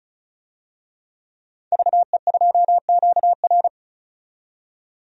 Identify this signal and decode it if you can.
Morse code, heard as a keyed tone, about 36 wpm, 700 Hz. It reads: VE2QR
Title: DXCC-100 Call Signs - 5 Characters - 35wpm